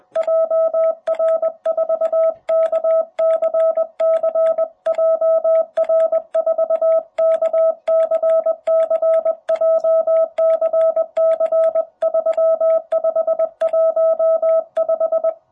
Morse code, heard as a keyed tone, about 20 wpm, 660 Hz. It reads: JR4KCC JR4KCC JCC3515
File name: jh8cbh / 2021cw